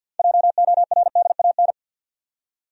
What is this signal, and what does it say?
Morse code, heard as a keyed tone, about 48 wpm, 700 Hz. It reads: JORDAN
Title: USA Common Names - 50wpm